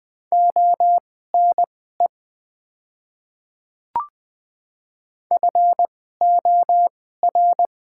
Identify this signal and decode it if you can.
Morse code, heard as a keyed tone, about 20 wpm, spaced as for 14 wpm, 690 Hz. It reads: ONE FOR
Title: 2x Character Spacing - Top 100 Words - 20wpm